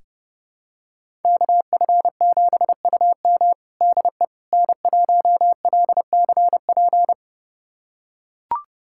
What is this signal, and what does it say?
Morse code, heard as a keyed tone, about 32 wpm, 700 Hz. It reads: KF7UM DE N1LCP
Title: USA Callsign DE Callsign - 5 Characters - 30wpm